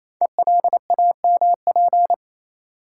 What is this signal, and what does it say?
Morse code, heard as a keyed tone, about 27 wpm, 700 Hz. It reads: ELAMP
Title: Code Groups - 5 Letters - 28wpm